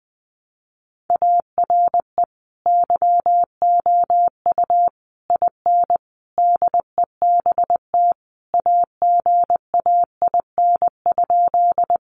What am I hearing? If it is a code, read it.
Morse code, heard as a keyed tone, about 20 wpm, 700 Hz. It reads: ARE YOU IN DEBT AGAIN?
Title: Sentences from Top 1300 Words 20wpm